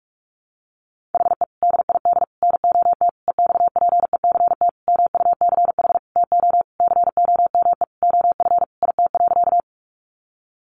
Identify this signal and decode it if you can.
Morse code, heard as a keyed tone, about 44 wpm, 700 Hz. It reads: HE DID NOT EXPECT MUCH TO COME OF IT.
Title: Sentences from Top 1100 Words 45wpm